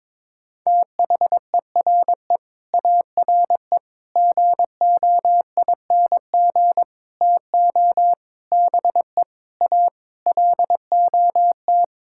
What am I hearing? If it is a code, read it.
Morse code, heard as a keyed tone, about 22 wpm, 700 Hz. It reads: THERE ARE GOING TO BE A LOT
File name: practice / 22wpm / Sets of 8 Words - 22wpm